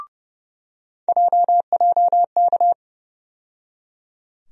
Morse code, heard as a keyed tone, about 32 wpm, 700 Hz. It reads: JJK
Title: USA Call Sign Postfixes - 3 Letters 30wpm